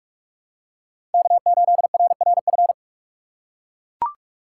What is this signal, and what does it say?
Morse code, heard as a keyed tone, about 43 wpm, 700 Hz. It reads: K8RRF
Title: USA Call Signs - 5 Characters - 4x Character Spacing - 45wpm